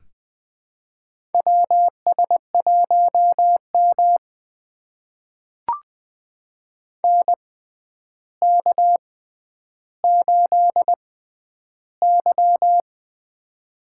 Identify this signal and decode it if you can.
Morse code, heard as a keyed tone, about 20 wpm, spaced as for 10 wpm, 700 Hz. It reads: WS1M NK8Y
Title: USA Call Signs - 4 Characters - 6x Character Spacing - 20wpm